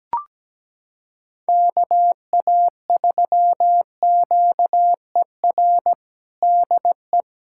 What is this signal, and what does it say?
Morse code, heard as a keyed tone, about 17 wpm, 700 Hz. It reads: KA3QER DE
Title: USA Callsign DE Callsign - 6 Characters - 17wpm